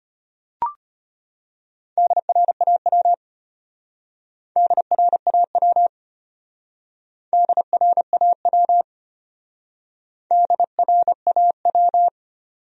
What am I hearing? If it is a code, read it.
Morse code, heard as a keyed tone, about 31 wpm, 700 Hz. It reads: DRAW DRAW DRAW DRAW
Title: Speed-Racing Top 500 Words 25wpm